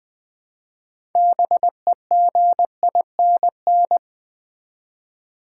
Morse code, heard as a keyed tone, about 20 wpm, 700 Hz. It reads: BEGINN
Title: Build A Word - 3 Letters At A Time - 9-Letter Words - 20wpm